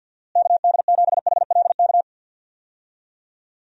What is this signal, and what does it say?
Morse code, heard as a keyed tone, about 47 wpm, 700 Hz. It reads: KD7HLK
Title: USA Call Signs - 6 Characters - Repeated 2x - 50wpm